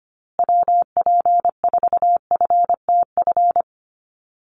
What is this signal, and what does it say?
Morse code, heard as a keyed tone, about 25 wpm, 700 Hz. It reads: WP4FTF
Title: USA Call Signs Repeated 3x 25wpm